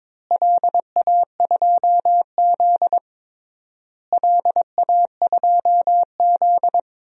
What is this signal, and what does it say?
Morse code, heard as a keyed tone, about 22 wpm, 700 Hz. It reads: LA2Z LA2Z
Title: DXCC-100 Call Signs - 4 Characters - Repeated 3x - 22wpm